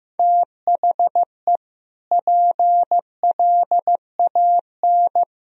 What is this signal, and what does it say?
Morse code, heard as a keyed tone, about 15 wpm, 700 Hz. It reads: THE PLAN
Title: Sentences from Top 1000 Words - Review Part 3 - 15wpm